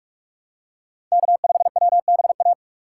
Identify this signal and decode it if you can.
Morse code, heard as a keyed tone, about 41 wpm, 700 Hz. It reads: K5WBA